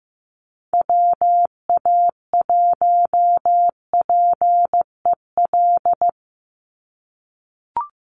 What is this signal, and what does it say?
Morse code, heard as a keyed tone, about 15 wpm, 700 Hz. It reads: WA1PEL
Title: USA Call Signs 15wpm